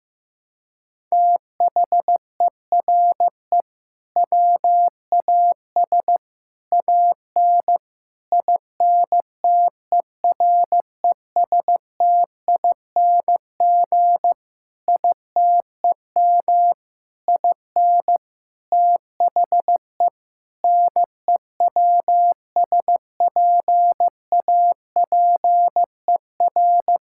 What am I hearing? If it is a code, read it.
Morse code, heard as a keyed tone, about 15 wpm, 700 Hz. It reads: THERE WAS AN INTERESTING ITEM IN THE NEWSPAPER
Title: Sentences from Top 1000 Words - Review Part 6 - 15wpm